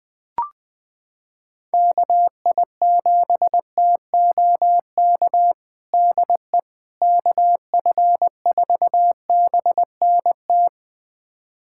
Morse code, heard as a keyed tone, about 21 wpm, 700 Hz. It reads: KI7TOK DE KF4BNT
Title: USA Callsign DE Callsign - 6 Characters - 20wpm